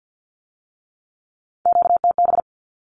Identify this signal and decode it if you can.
Morse code, heard as a keyed tone, about 50 wpm, 700 Hz. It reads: QTB
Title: Q-Codes with Spoken Meaning - 50wpm